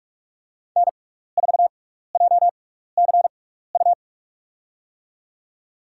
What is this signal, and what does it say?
Morse code, heard as a keyed tone, about 41 wpm, spaced as for 15 wpm, 700 Hz. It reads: N4JCU
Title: USA Call Signs - 5 Characters - 6x Character Spacing - 45wpm